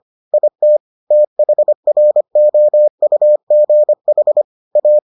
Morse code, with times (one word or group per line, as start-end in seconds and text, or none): 0.3-0.8 IT
1.1-4.4 THROUGH
4.8-5.0 A